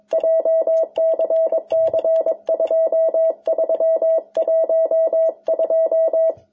Morse code, heard as a keyed tone, about 22 wpm, 650 Hz. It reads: JCC 2312
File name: jh8cbh / 2021cw